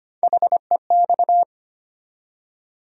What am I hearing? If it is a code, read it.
Morse code, heard as a keyed tone, about 25 wpm, 700 Hz. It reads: HEX
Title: USA Call Sign Postfixes - 3 Letters 25wpm